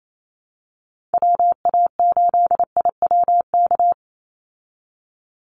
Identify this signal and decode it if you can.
Morse code, heard as a keyed tone, about 28 wpm, 700 Hz. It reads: WA8IWK
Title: Call Signs Repeated 2x - 28wpm